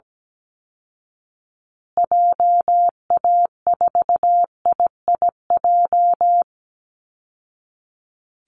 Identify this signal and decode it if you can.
Morse code, heard as a keyed tone, about 17 wpm, 700 Hz. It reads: JA4IIJ
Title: Call Signs Repeated 2x - 17wpm